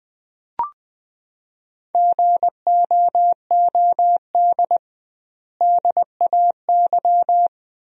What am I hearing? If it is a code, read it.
Morse code, heard as a keyed tone, about 20 wpm, 700 Hz. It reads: GOOD DAY
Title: 2x Word-Spacing - Sentences from Top 200 Words - 20wpm